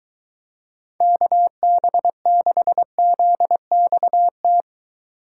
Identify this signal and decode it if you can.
Morse code, heard as a keyed tone, about 23 wpm, 700 Hz. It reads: KB6ZXT